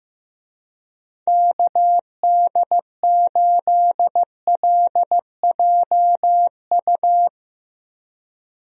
Morse code, heard as a keyed tone, about 16 wpm, 700 Hz. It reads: KD8LJU
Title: USA Call Signs - 6 Characters - 15wpm